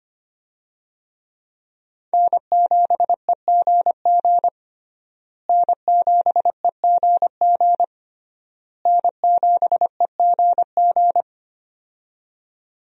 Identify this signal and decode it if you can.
Morse code, heard as a keyed tone, about 25 wpm, 700 Hz. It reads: N7EGG N7EGG N7EGG
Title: USA Call Signs - 5 Characters - Repeated 3x - 25wpm